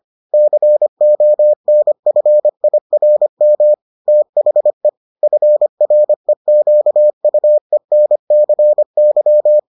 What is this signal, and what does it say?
Morse code, heard as a keyed tone, about 25 wpm, 600 Hz. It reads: CONFIRM THE FREQUENCY